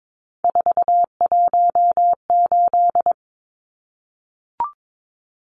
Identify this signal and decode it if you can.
Morse code, heard as a keyed tone, about 22 wpm, 700 Hz. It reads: 418
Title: Numbers - 3-Digit 22wpm